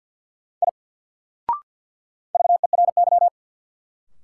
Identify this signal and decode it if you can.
Morse code, heard as a keyed tone, about 49 wpm, 700 Hz. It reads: I VERY
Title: Sentences from Top 200 Words - New Word Drill - 50wpm